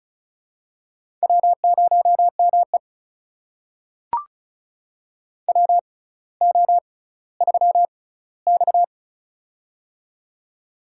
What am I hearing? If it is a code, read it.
Morse code, heard as a keyed tone, about 36 wpm, spaced as for 18 wpm, 700 Hz. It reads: W0ME WO3X
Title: USA Call Signs - 4 Characters - 6x Character Spacing - 35wpm